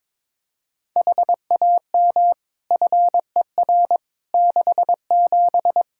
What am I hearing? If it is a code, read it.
Morse code, heard as a keyed tone, about 22 wpm, 700 Hz. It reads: HAM FER 67
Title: Practice QSO - Exchange 3 - More Info - 22wpm